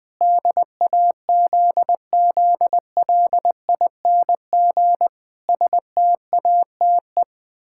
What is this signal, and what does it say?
Morse code, heard as a keyed tone, about 20 wpm, 700 Hz. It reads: DAZZLING STATE
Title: Pangram Challenge v2 - 20wpm